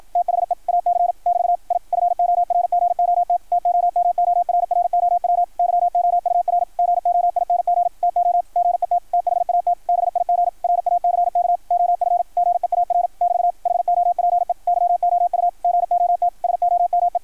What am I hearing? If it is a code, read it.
Morse code, heard as a keyed tone, about 53 wpm, 690 Hz, with radio noise. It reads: THE AC <BT> I FORGOT TOMORROW YOUR GOING TO GET THAT BIG RACK OF GEAR <BT> HOPE YOU GOT SOME
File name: archived / CRSnet